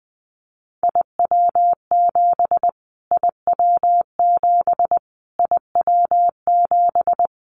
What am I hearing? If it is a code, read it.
Morse code, heard as a keyed tone, about 20 wpm, 700 Hz. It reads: IW7 IW7 IW7
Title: Callsign Prefixes for Top 25 DXCC - Repeated 3x - 20wpm